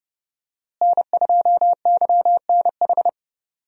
Morse code, heard as a keyed tone, about 29 wpm, 700 Hz. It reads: N2YNH
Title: USA Call Signs - 5 Characters - Repeated 3x - 30wpm